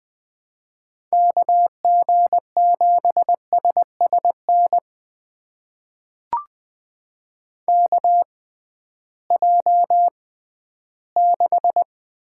Morse code, heard as a keyed tone, about 20 wpm, spaced as for 13 wpm, 700 Hz. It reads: KG7SSN KJ6
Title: USA Call Signs - 6 Characters - 6x Character Spacing - 20wpm